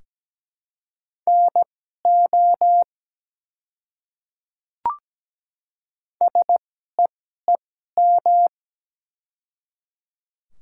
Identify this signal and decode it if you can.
Morse code, heard as a keyed tone, about 17 wpm, spaced as for 12 wpm, 700 Hz. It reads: NO SEEM
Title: 2x Character Spacing - Top 300 Words - 17wpm